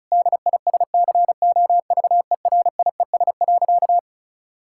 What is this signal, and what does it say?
Morse code, heard as a keyed tone, about 34 wpm, 700 Hz. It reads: DISCOVERIES.
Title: Pangram Challenge v2 - 35wpm